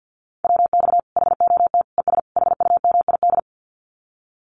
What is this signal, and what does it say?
Morse code, heard as a keyed tone, about 50 wpm, 700 Hz. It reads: WX HOT ES HUMID
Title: Common QSO Phrases 50wpm